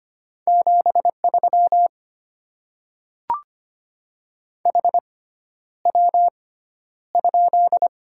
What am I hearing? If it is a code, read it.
Morse code, heard as a keyed tone, about 25 wpm, spaced as for 11 wpm, 700 Hz. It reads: 73 HW?
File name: Top 20 QSO Elements - 6x Character Spacing - 25wpm